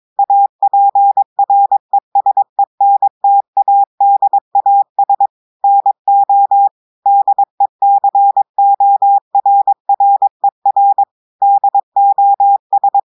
Morse code, heard as a keyed tone, about 22 wpm, 800 Hz. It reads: APRESENTADAS NO DECORRER DOS
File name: audios / portugais / portugais texte en clair 2 vitesse 22 mots minutes